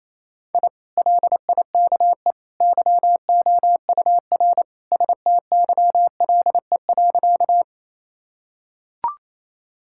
Morse code, heard as a keyed tone, about 28 wpm, 700 Hz. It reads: I LIKE YOUR STYLE.
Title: Sentences from Top 1000 Words - Review Part 5 - 28wpm